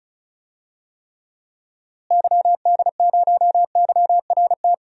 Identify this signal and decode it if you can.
Morse code, heard as a keyed tone, about 36 wpm, 700 Hz. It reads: YD0YRT
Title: DXCC-100 Call Signs - 6 Characters - Repeated 2x - 35wpm